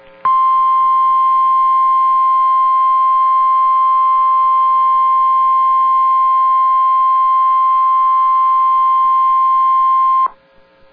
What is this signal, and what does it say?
Emergency Alert System (SAME) tones (the 1050 Hz attention tone). It is heard from 0.2 to 10.3 s.